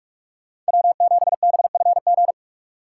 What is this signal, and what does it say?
Morse code, heard as a keyed tone, about 59 wpm, 700 Hz. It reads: W7BFG